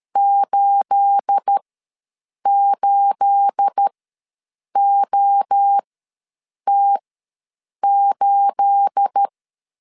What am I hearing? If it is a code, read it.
Morse code, heard as a keyed tone, about 13 wpm, spaced as for 7 wpm, 800 Hz. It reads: 88OT8
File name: LEZIONI / 9 28 ws 12wpm 3354